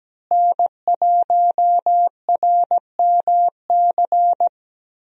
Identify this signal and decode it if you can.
Morse code, heard as a keyed tone, about 17 wpm, 700 Hz. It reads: N1RMC
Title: USA Call Signs - 5 Characters - Repeated 3x - 17wpm